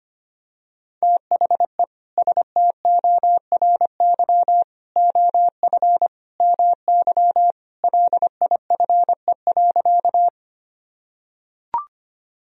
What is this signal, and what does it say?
Morse code, heard as a keyed tone, about 25 wpm, 700 Hz. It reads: THE STORY OF MY LIFE.
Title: Sentences from Top 1000 Words - Review Part 1 - 25wpm